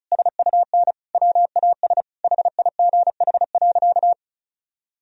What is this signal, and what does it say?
Morse code, heard as a keyed tone, about 36 wpm, 700 Hz. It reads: SUN WAS HIGH.
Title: Sentences from Top 1000 Words - Review Part 2 - 35wpm